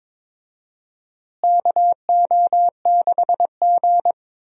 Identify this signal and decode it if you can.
Morse code, heard as a keyed tone, about 22 wpm, 700 Hz. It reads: KO6G
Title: USA Call Signs - 4 Characters - Repeated 3x - 22wpm